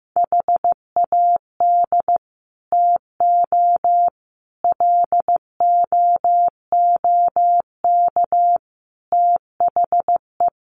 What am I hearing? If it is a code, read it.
Morse code, heard as a keyed tone, about 15 wpm, 700 Hz. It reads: HAD TO LOOK THE
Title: Sentences from Top 100 Words 15wpm